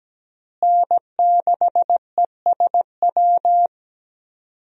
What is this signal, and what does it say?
Morse code, heard as a keyed tone, about 17 wpm, 700 Hz. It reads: N6ESW
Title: USA Call Signs - 5 Characters - Repeated 2x - 17wpm